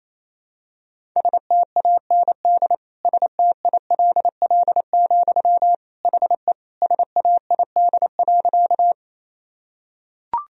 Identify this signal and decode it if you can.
Morse code, heard as a keyed tone, about 28 wpm, 700 Hz. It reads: STAND STILL, HE SAID.
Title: Sentences from Top 1000 Words - Review Part 1 - 28wpm